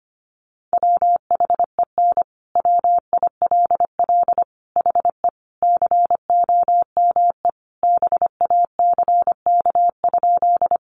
Morse code, heard as a keyed tone, about 25 wpm, 700 Hz. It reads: WHEN WILL HE COME BACK?
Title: Sentences from Top 200 Words 25wpm